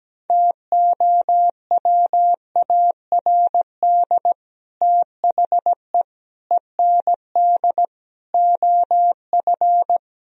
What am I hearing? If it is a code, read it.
Morse code, heard as a keyed tone, about 17 wpm, 700 Hz. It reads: TOWARD THE END OF
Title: Speed-Racing - Sets of 5 Words 17wpm